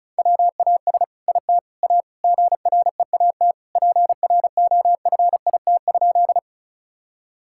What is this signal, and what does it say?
Morse code, heard as a keyed tone, about 34 wpm, 690 Hz. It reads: WAS IT A GREAT PROFIT?
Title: Sentences from Top 1000 Words - Review Part 5 - 35wpm